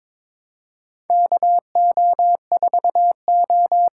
Morse code, heard as a keyed tone, about 22 wpm, 700 Hz. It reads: KO4O
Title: USA Call Signs - 8x Character Spacing - 22wpm